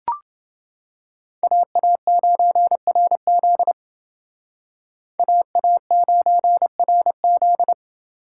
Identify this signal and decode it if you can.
Morse code, heard as a keyed tone, about 29 wpm, 700 Hz. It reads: AA9RZ AA9RZ